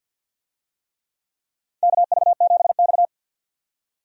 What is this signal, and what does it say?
Morse code, heard as a keyed tone, about 52 wpm, 700 Hz. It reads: KV7X
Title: USA Call Signs - 4 Characters - 50wpm